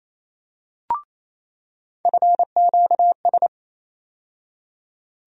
Morse code, heard as a keyed tone, about 27 wpm, 700 Hz. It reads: FQS
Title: USA Call Sign Postfixes - 3 Letters 28wpm